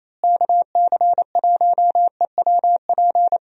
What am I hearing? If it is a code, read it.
Morse code, heard as a keyed tone, about 28 wpm, 700 Hz. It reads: KC1EWP
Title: USA Call Signs - 6 Characters - Repeated 2x - 28wpm